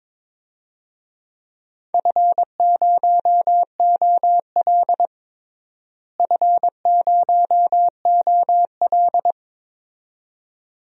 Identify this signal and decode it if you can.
Morse code, heard as a keyed tone, about 22 wpm, 700 Hz. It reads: F0OL F0OL
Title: DXCC-100 Call Signs - 4 Characters - Repeated 2x - 22wpm